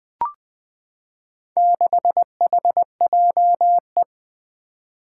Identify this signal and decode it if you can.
Morse code, heard as a keyed tone, about 20 wpm, 700 Hz. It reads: 6HJE
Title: Four Challenging Letter-Number Combinations 20wpm